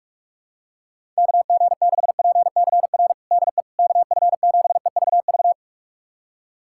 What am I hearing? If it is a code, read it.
Morse code, heard as a keyed tone, about 41 wpm, 700 Hz. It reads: KG6PCR DE KF7EVV
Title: USA Callsign DE Callsign - 6 Characters - 45wpm